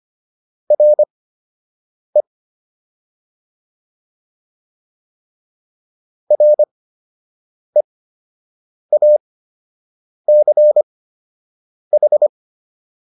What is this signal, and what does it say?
Morse code, heard as a keyed tone, about 25 wpm, spaced as for 7 wpm, 600 Hz. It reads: RE REACH